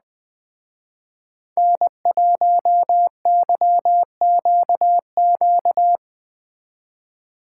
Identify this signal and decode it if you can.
Morse code, heard as a keyed tone, about 20 wpm, 700 Hz. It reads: N1YQQ
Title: Speed Racing - USA 1x3 Call Signs 17wpm